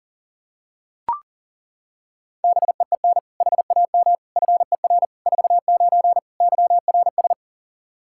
Morse code, heard as a keyed tone, about 40 wpm, 700 Hz. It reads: BEEN HAM FER 49 YRS
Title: Practice QSO - Exchange 3 - More Info - 40wpm